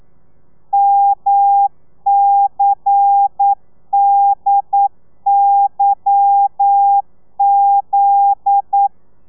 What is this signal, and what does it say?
Morse code, heard as a keyed tone, about 9 wpm, 800 Hz. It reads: MCDYZ